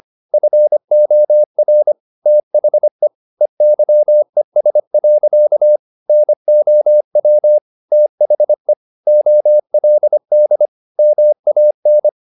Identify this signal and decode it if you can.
Morse code, heard as a keyed tone, about 25 wpm, 600 Hz. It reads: FOR THE EYES. NOW THE OLD MAN
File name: the old man and the sea part 1 25wpm